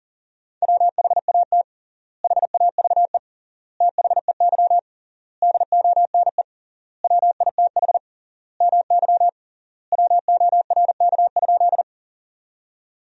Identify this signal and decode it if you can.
Morse code, heard as a keyed tone, about 37 wpm, 700 Hz. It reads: WHAT HAVE THEY DONE WITH MY WORK?
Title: 3x Word-Spacing - Sentences from Top 500 Words - 40wpm